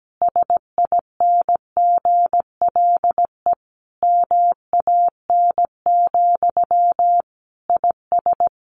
Morse code, heard as a keyed tone, about 17 wpm, 700 Hz. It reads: SINGLE MAN, IS
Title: Sentences from Top 1400 Words 17wpm